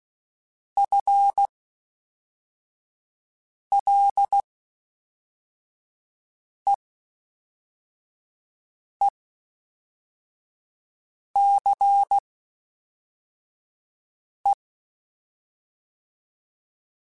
Morse code, heard as a keyed tone, about 16 wpm, spaced as for 4 wpm, 800 Hz. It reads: FLEECE
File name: Spaced / FL Runs sp mp3s / FL Runs sp @5